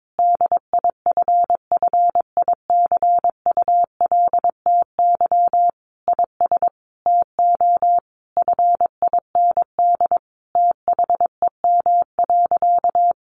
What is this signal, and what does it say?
Morse code, heard as a keyed tone, about 22 wpm, 700 Hz. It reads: DIFFICULTY IS TO FIND THEM.
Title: Sentences from Top 1500 Words 22wpm